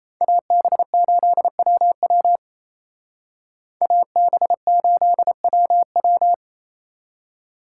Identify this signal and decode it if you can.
Morse code, heard as a keyed tone, about 34 wpm, 700 Hz. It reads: AB8WW AB8WW
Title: Speed Racing - USA 2x2 Call Signs 28wpm